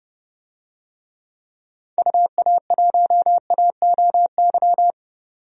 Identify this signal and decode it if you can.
Morse code, heard as a keyed tone, about 32 wpm, 700 Hz. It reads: UA1AOY
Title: DXCC-100 Call Signs - 6 Characters - 30wpm